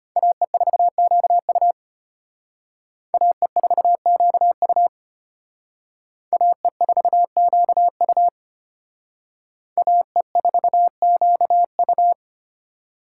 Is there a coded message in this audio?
Yes: Morse code from 0.2 to 12.1 s, heard as a keyed tone, about 33 wpm, 700 Hz. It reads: AE4QU AE4QU AE4QU AE4QU